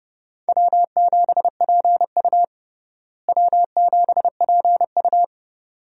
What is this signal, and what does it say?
Morse code, heard as a keyed tone, about 31 wpm, 700 Hz. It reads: W7PU W7PU
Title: USA Call Signs - 4 Characters - Repeated 2x - 30wpm